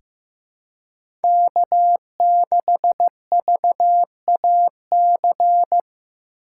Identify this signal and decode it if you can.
Morse code, heard as a keyed tone, about 15 wpm, 700 Hz. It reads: K6VAC
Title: USA Call Signs - 5 Characters - 2x Character Spacing - 15wpm